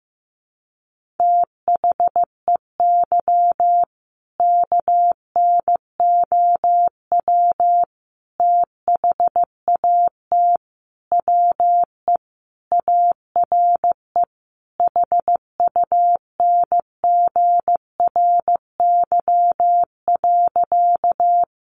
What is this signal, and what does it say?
Morse code, heard as a keyed tone, about 15 wpm, 700 Hz. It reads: THEY KNOW THAT WE ARE HUNGRY.
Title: Sentences from Top 1400 Words 15wpm